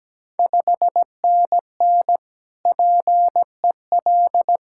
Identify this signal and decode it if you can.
Morse code, heard as a keyed tone, about 17 wpm, 700 Hz. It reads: 5NN PEL
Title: Ontario QSO Party Exchange 17wpm